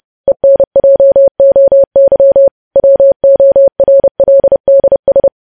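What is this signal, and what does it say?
Morse code, heard as a keyed tone, about 30 wpm, 550 Hz. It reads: ENJOY WORLDS
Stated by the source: Morse news headlines sent at 30 wpm.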